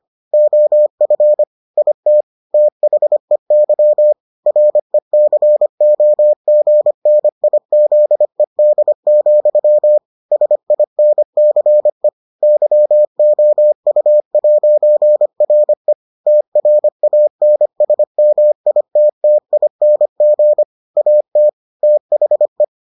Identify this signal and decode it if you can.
Morse code, heard as a keyed tone, about 25 wpm, 600 Hz. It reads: OF IT THEY RECOGNIZED, SINCE YOU'RE TRANSMITTING AT THE